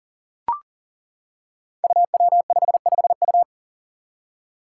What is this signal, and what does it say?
Morse code, heard as a keyed tone, about 40 wpm, 700 Hz. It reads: UW55U
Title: Five Challenging Letter-Number Combinations 40wpm